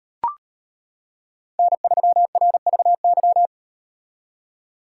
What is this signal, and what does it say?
Morse code, heard as a keyed tone, about 38 wpm, 700 Hz. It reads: N3RVY